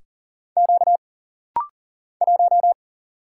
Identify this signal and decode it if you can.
Morse code, heard as a keyed tone, about 40 wpm, 700 Hz. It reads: Q 1
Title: Single Letter-Number Alt - Rapid-Fire 40wpm